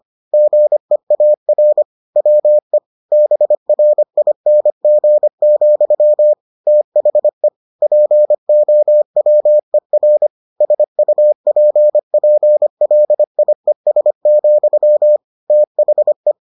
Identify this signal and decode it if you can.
Morse code, heard as a keyed tone, about 25 wpm, 600 Hz. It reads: GEAR WE BRING, THE POWER SUPPLIES, THE